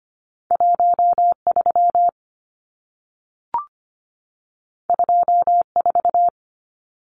Morse code, heard as a keyed tone, about 25 wpm, 700 Hz. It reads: 13 24
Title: Numbers - 2-Digit 25wpm